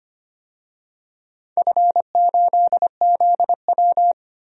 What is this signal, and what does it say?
Morse code, heard as a keyed tone, about 25 wpm, 700 Hz. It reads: F8ZW
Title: DXCC-100 Call Signs - 4 Characters - 25wpm